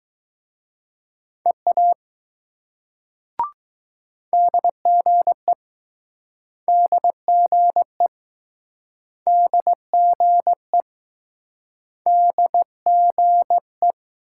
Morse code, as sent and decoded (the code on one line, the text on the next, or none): . .- / -.. --. / -.. --. / -.. --. / -.. --. .
EA DGE DGE DGE DGE